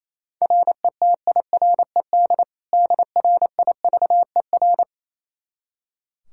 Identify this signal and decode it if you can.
Morse code, heard as a keyed tone, about 28 wpm, 700 Hz. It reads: RETIRED DRIVER
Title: Practice QSO - Exchange 3 - More Info - 28wpm